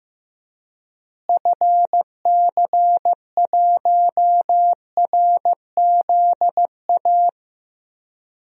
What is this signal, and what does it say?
Morse code, heard as a keyed tone, about 15 wpm, 700 Hz. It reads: FC1RZA
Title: DXCC-100 Call Signs - 6 Characters - Repeated 2x - 15wpm